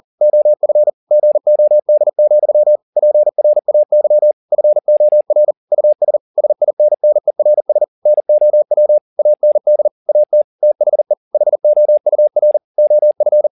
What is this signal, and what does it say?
Morse code, heard as a keyed tone, about 40 wpm, 600 Hz. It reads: OF GOD, PRAY FOR US SINNERS NOW AND AT THE HOUR OF